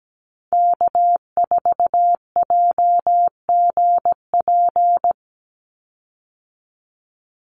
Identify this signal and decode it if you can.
Morse code, heard as a keyed tone, about 17 wpm, 700 Hz. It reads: K4JGP
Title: USA Call Signs 17wpm